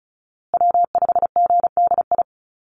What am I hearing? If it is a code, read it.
Morse code, heard as a keyed tone, about 34 wpm, 700 Hz. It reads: W5GDI
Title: USA Call Signs Repeated 2x 35wpm